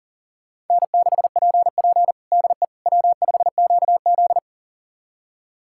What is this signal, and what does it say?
Morse code, heard as a keyed tone, about 40 wpm, 700 Hz. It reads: N6PP DE W5QZ
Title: USA Callsign DE Callsign - 4 Characters - 40wpm